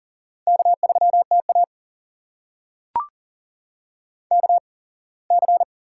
Morse code, heard as a keyed tone, about 40 wpm, spaced as for 24 wpm, 700 Hz. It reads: K3TA KC
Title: USA Call Signs - 8x Character Spacing - 40wpm